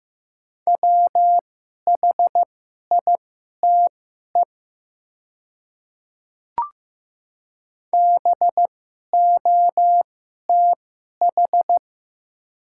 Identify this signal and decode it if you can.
Morse code, heard as a keyed tone, about 15 wpm, spaced as for 11 wpm, 690 Hz. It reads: WHITE BOTH
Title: 2x Character Spacing - Top 500 Words - 15wpm